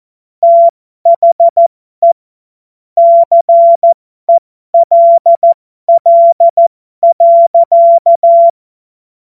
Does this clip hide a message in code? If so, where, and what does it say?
Yes: Morse code from 0.4 to 8.5 s, heard as a keyed tone, about 14 wpm, spaced as for 12 wpm, 680 Hz. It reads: THE CELL.